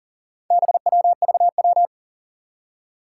Morse code, heard as a keyed tone, about 41 wpm, 700 Hz. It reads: BWVW
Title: Four Challenging Letter-Number Combinations 40wpm